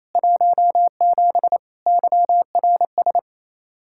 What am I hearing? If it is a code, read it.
Morse code, heard as a keyed tone, about 28 wpm, 700 Hz. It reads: 17 YRS
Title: Practice QSO - Exchange 3 - More Info - 28wpm